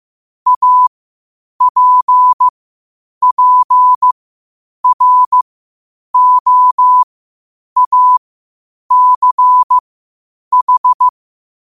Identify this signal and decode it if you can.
Morse code, heard as a keyed tone, about 15 wpm, spaced as for 8 wpm, 1000 Hz. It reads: APPROACH
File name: qotdfiles / 8wpm